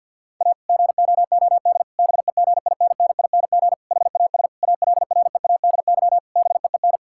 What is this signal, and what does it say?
Morse code, heard as a keyed tone, about 50 wpm, 700 Hz. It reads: A GOOD BEGINNING HAS ALREADY BEEN
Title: Sentences from Top 1000 Words - Review Part 5 - 50wpm